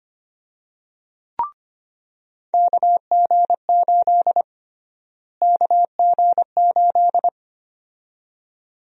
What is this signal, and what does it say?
Morse code, heard as a keyed tone, about 25 wpm, 700 Hz. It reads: KG8 KG8
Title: USA Call Sign Prefixes - Repeated 2x - 25wpm